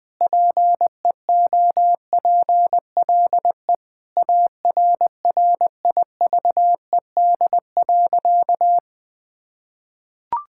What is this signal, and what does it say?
Morse code, heard as a keyed tone, about 20 wpm, 700 Hz. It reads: PEOPLE ARRIVED.
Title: Sentences with 3-Word Phrases - 20wpm